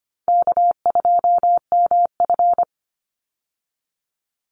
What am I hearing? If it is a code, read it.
Morse code, heard as a keyed tone, about 25 wpm, 700 Hz. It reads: K2MF
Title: Easy Call Signs Repeated 3x - 25wpm